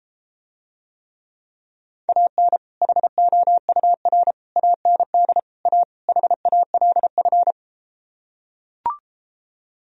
Morse code, heard as a keyed tone, about 33 wpm, 700 Hz. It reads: AN HOUR AND A HALF